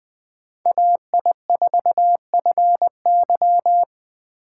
Morse code, heard as a keyed tone, about 20 wpm, 700 Hz. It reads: AI4FY